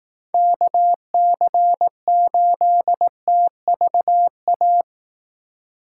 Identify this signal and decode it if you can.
Morse code, heard as a keyed tone, about 18 wpm, 700 Hz. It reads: KC8TVA
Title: Speed Racing - USA Call Signs 15wpm